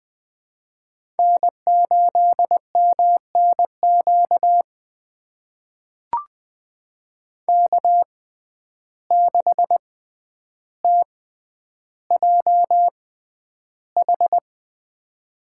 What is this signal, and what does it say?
Morse code, heard as a keyed tone, about 21 wpm, spaced as for 10 wpm, 700 Hz. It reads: N8MNQ K6TJH
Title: USA Call Signs - 6x Character Spacing - 20wpm